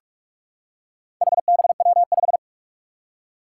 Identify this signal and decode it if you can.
Morse code, heard as a keyed tone, about 44 wpm, 700 Hz. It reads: HBW5
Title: Four Challenging Letter-Number Combinations 45wpm